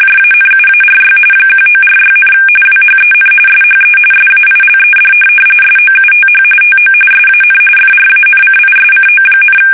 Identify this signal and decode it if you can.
RTTY, about 45.45 baud. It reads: JANUARY 1946, WHEN SHE SAILED SOUTH TO SAN FRANCISCO,
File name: fsk850 / USS-Massachusetts